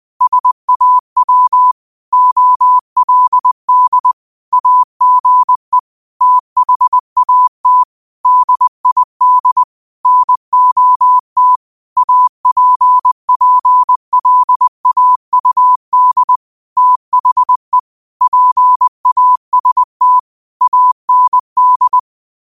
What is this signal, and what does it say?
Morse code, heard as a keyed tone, about 20 wpm, 1000 Hz. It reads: SAW OLD AGE THAT DID NOT APPLAUD THE PAST AND
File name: qotdfiles / 20wpm